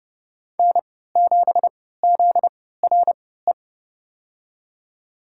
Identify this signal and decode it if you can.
Morse code, heard as a keyed tone, about 30 wpm, spaced as for 17 wpm, 700 Hz. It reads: N7ZRE
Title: USA Call Signs - 5 Characters - 3x Character Spacing - 30wpm